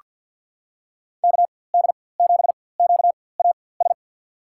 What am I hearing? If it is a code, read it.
Morse code, heard as a keyed tone, about 47 wpm, spaced as for 23 wpm, 700 Hz. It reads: KD7QAS
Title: USA Call Signs - 6 Characters - 4x Character Spacing - 50wpm